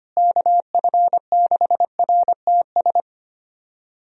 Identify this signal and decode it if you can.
Morse code, heard as a keyed tone, about 25 wpm, 700 Hz. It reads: KF6RTS
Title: USA Call Signs - 6 Characters - Repeated 2x - 25wpm